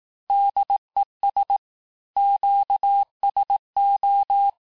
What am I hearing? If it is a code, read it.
Morse code, heard as a keyed tone, about 18 wpm, 800 Hz. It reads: DES QSO
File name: cours-du-lundi files / cours-1588-FIN-de-QSO-vitesse-17MPM